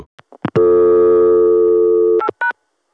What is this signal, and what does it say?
Dial tone (North America), 350 + 440 Hz.